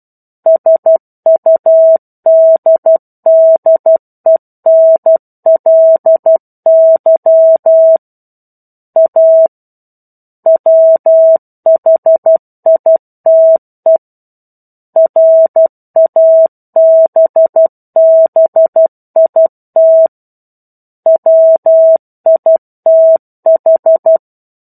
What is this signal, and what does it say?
Morse code, heard as a keyed tone, about 12 wpm, 650 Hz. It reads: SUDDENLY A WHITE RABBIT WITH
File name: 01-Chapter-01 12 WPM